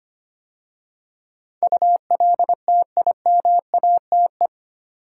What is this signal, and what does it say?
Morse code, heard as a keyed tone, about 25 wpm, 700 Hz. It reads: ULTIMATE
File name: 8 Letter Words - First 3 Letters Given - 25wpm